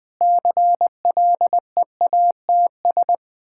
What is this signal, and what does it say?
Morse code, heard as a keyed tone, about 21 wpm, 700 Hz. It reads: CLEATS
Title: Difficult Words with Hints - 20wpm